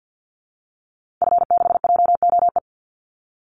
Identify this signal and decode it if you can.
Morse code, heard as a keyed tone, about 50 wpm, 700 Hz. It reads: F6JOE
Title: Call Signs - 50wpm